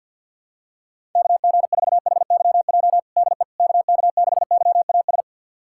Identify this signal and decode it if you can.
Morse code, heard as a keyed tone, about 52 wpm, 700 Hz. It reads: KG4HYJ DE KK6YAS
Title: USA Callsign DE Callsign - 6 Characters - 50wpm